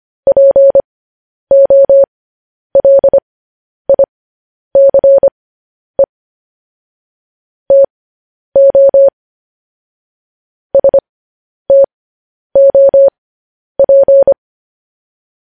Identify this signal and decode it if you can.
Morse code, heard as a keyed tone, about 25 wpm, spaced as for 10 wpm, 550 Hz. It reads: POLICE TO STOP
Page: news-headlines-in-morse-code-at-10-wpm